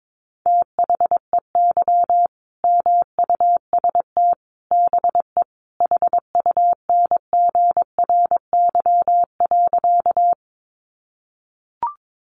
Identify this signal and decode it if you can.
Morse code, heard as a keyed tone, about 22 wpm, 700 Hz. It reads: THEY MUST BE HUNGRY.
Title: Sentences from Top 1400 Words 22wpm